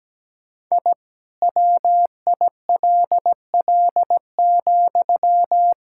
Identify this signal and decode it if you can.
Morse code, heard as a keyed tone, about 17 wpm, 700 Hz. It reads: I WILL,
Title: Sentences from Top 1000 Words - Review Part 5 - 17wpm